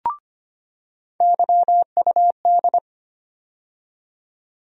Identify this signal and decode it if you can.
Morse code, heard as a keyed tone, about 25 wpm, 700 Hz. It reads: YUD
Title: USA Call Sign Postfixes - 3 Letters 25wpm